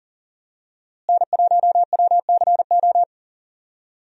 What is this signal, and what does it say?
Morse code, heard as a keyed tone, about 40 wpm, 700 Hz. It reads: N1WCO
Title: USA Call Signs - 5 Characters - Repeated 3x - 40wpm